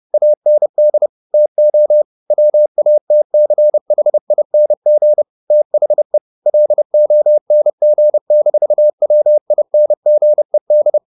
Morse code, heard as a keyed tone, about 30 wpm, 600 Hz. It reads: AND TO WATCHING THE LONG-WINGED
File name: the old man and the sea / the old man and the sea 1 30wpm